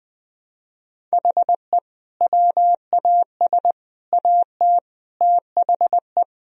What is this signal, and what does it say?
Morse code, heard as a keyed tone, about 20 wpm, 700 Hz. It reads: HE WAS AT THE
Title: Sentences from Top 1000 Words - Review Part 2 - 20wpm